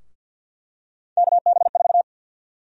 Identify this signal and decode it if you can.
Morse code, heard as a keyed tone, about 51 wpm, 700 Hz. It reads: KB4
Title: USA Call Sign Prefixes - Repeated 2x - 50wpm